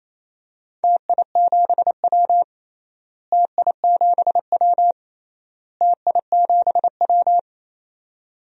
Morse code, heard as a keyed tone, about 28 wpm, 700 Hz. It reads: TI7W TI7W TI7W
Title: DXCC-100 Call Signs - 4 Characters - Repeated 3x - 28wpm